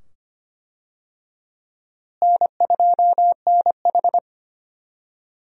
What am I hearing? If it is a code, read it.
Morse code, heard as a keyed tone, about 25 wpm, 700 Hz. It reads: N2NH
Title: USA Call Signs - 4 Characters - Repeated 2x - 25wpm